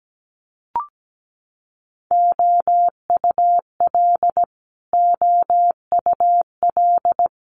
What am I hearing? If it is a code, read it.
Morse code, heard as a keyed tone, about 17 wpm, 700 Hz. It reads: OUL OUL
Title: Multi-letter Phonemes Repeated 3x 17wpm